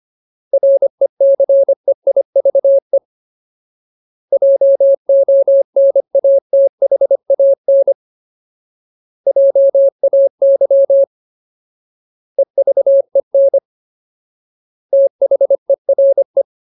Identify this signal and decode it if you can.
Morse code, heard as a keyed tone, about 25 wpm, 550 Hz. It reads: RECEIVE JONATHAN JAY EVEN THERE
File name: Ultimate Word List / 25WPM Ultimate